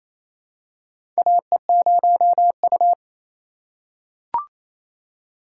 Morse code, heard as a keyed tone, about 28 wpm, 700 Hz. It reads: AE0U
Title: USA Call Signs - 4 Characters - 4x Character Spacing - 28wpm